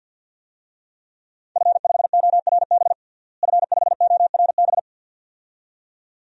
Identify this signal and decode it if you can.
Morse code, heard as a keyed tone, about 48 wpm, 700 Hz. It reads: F5ORB F5ORB
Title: DXCC-100 Call Signs - 5 Characters - Repeated 2x - 50wpm